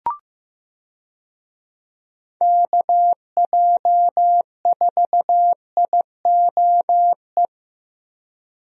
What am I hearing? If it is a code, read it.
Morse code, heard as a keyed tone, about 15 wpm, 700 Hz. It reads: KJ4IOE
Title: USA Call Signs - 6 Characters - Repeated 2x - 15wpm